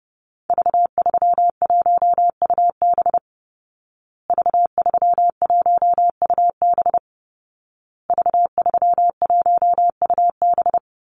Morse code, heard as a keyed tone, about 30 wpm, 700 Hz. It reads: V31UB V31UB V31UB
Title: Call Signs Repeated 2x - 30wpm